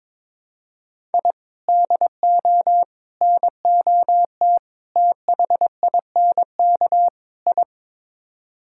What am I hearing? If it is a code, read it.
Morse code, heard as a keyed tone, about 22 wpm, 700 Hz. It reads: I DO NOT THINK I
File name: practice / 22wpm / Build A Phrase - 5-Word Phrases - 22wpm